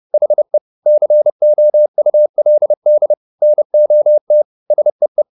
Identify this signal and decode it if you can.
Morse code, heard as a keyed tone, about 30 wpm, 600 Hz. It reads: HE COULD NOT SEE